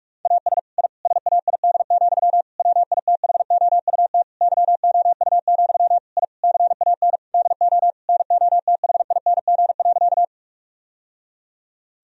Morse code, heard as a keyed tone, about 46 wpm, 700 Hz. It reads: AS I SAID, WITHOUT YOU, I CAN DO NOTHING.
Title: Sentences from Top 1000 Words - Review Part 3 - 45wpm